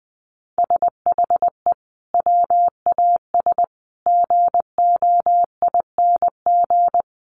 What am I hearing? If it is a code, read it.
Morse code, heard as a keyed tone, about 20 wpm, 700 Hz. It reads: SHE WAS GOING
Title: Sets of 4 Words 20wpm